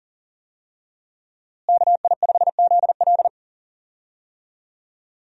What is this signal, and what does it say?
Morse code, heard as a keyed tone, about 40 wpm, 700 Hz. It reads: KI5ZL
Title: USA Call Signs - 5 Characters - Repeated 2x - 40wpm